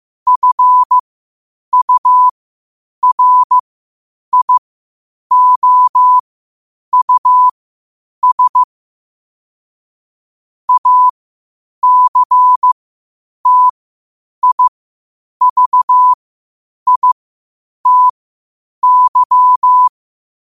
Morse code, heard as a keyed tone, about 15 wpm, spaced as for 8 wpm, 1000 Hz. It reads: FURIOUS ACTIVITY
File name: qotdfiles / 8wpm